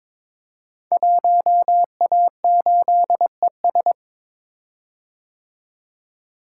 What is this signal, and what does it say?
Morse code, heard as a keyed tone, about 22 wpm, 700 Hz. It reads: 1A8ES
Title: Code Groups - 5 Letter:Numbers - 22wpm